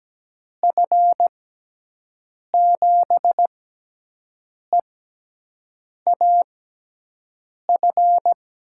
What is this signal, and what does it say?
Morse code, heard as a keyed tone, about 17 wpm, spaced as for 6 wpm, 700 Hz. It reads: F7EAF
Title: USA Call Signs - 6 Characters - 6x Character Spacing - 17wpm